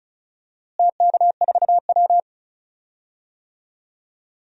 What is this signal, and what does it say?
Morse code, heard as a keyed tone, about 36 wpm, 700 Hz. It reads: TK4W
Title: DXCC-100 Call Signs - 4 Characters - Repeated 2x - 35wpm